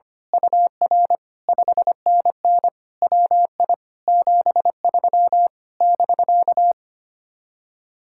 Morse code, heard as a keyed tone, about 24 wpm, 700 Hz. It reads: UR 5NN WI 73 <BK>
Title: Speed-Racing Common QSO Phrases - 25wpm